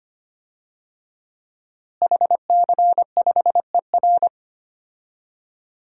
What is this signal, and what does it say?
Morse code, heard as a keyed tone, about 25 wpm, 700 Hz. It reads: HC5ER
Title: DXCC-100 Call Signs - 5 Characters - Repeated 2x - 25wpm